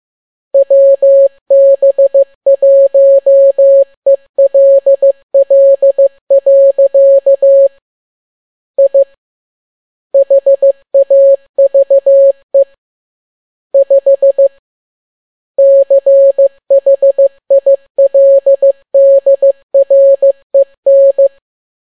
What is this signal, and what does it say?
Morse code, heard as a keyed tone, about 15 wpm, 550 Hz. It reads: WB1ELL. I HAVE 5 CHILDREN